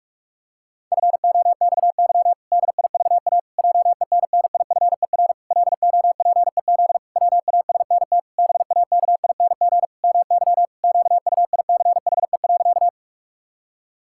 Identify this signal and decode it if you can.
Morse code, heard as a keyed tone, about 41 wpm, 700 Hz. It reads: FOXY DIVA JENNIFER LOPEZ WASNT BAKING MY QUICHE.
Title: Pangram Challenge v2 - 45wpm